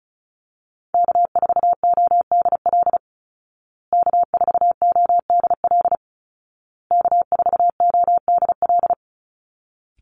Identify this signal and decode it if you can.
Morse code, heard as a keyed tone, about 34 wpm, 700 Hz. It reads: K4ODL K4ODL K4ODL
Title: Call Signs Repeated 2x - 35wpm